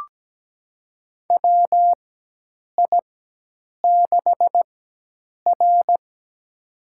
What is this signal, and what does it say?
Morse code, heard as a keyed tone, about 17 wpm, spaced as for 8 wpm, 700 Hz. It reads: WI6R
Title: USA Call Signs - 4 Characters - 4x Character Spacing - 17wpm